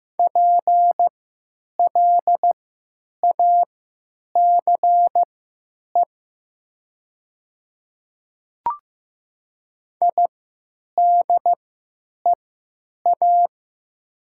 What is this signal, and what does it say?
Morse code, heard as a keyed tone, about 15 wpm, spaced as for 9 wpm, 700 Hz. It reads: PLACE IDEA
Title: 3x Character Spacing - Top 500 Words - 15wpm